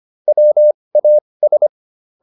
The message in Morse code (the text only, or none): WAS